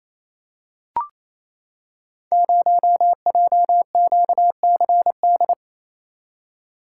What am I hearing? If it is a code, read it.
Morse code, heard as a keyed tone, about 27 wpm, 700 Hz. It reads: 0JQCD
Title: Code Groups - 5 Letter:Numbers - 28wpm